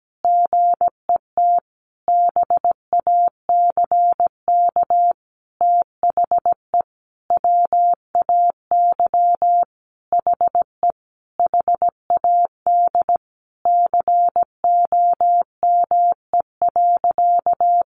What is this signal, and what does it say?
Morse code, heard as a keyed tone, about 17 wpm, 700 Hz. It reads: GET BACK THE WAY HE HAD COME.
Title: Sentences from Top 1100 Words 17wpm